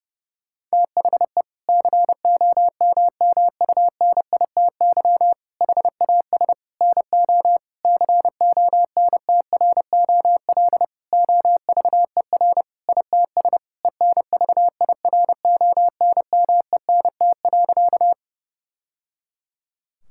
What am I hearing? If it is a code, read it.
Morse code, heard as a keyed tone, about 30 wpm, 700 Hz. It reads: THE COMMUNITY HAS NO CONTROL OVER ITS ENVIRONMENT.
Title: Sentences from Top 1000 Words - Review Part 5 - 30wpm